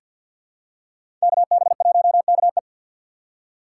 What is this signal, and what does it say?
Morse code, heard as a keyed tone, about 47 wpm, 700 Hz. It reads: KB1KE
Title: USA Call Signs - 5 Characters - 6x Character Spacing - 50wpm